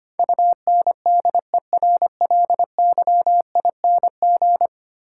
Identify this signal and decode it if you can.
Morse code, heard as a keyed tone, about 25 wpm, 700 Hz. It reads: UNDERLYING
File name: Ten Letter Words - 25wpm